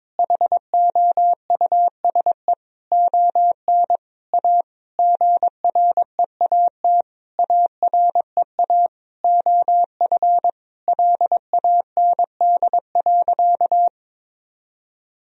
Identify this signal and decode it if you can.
Morse code, heard as a keyed tone, about 22 wpm, 700 Hz. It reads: HOUSE ON A GREAT AREA OF LAND.
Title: Sentences from Top 1000 Words - Review Part 2 - 22wpm